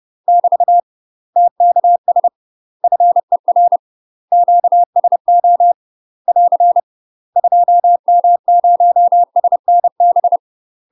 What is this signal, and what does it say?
Morse code, heard as a keyed tone, about 30 wpm, 700 Hz. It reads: <BT> TKS FER QSO <AR> 2M0SNB
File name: UK MASTER QSOs TRAINING / 30WPM / UK MASTER QSOs 0011